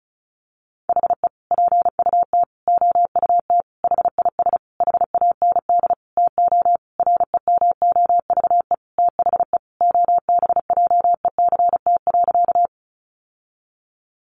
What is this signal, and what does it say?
Morse code, heard as a keyed tone, about 34 wpm, 700 Hz. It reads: HE PUT OUT HIS HAND TO REMOVE THE OBJECT.
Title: Sentences from Top 800 Words 35wpm